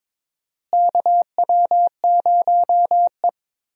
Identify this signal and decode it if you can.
Morse code, heard as a keyed tone, about 22 wpm, 700 Hz. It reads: KW0E
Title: USA Call Signs - 4 Characters - Repeated 3x - 22wpm